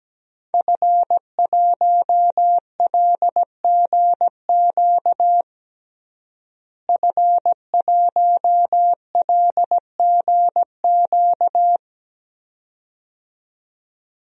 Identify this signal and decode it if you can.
Morse code, heard as a keyed tone, about 17 wpm, 700 Hz. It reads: F1LGQ F1LGQ
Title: DXCC-100 Call Signs - 5 Characters - Repeated 2x - 17wpm